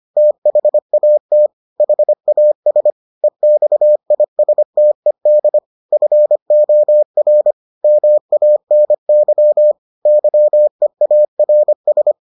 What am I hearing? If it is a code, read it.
Morse code, heard as a keyed tone, about 25 wpm, 600 Hz. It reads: THAT HAS EXISTED FOR MANY YEARS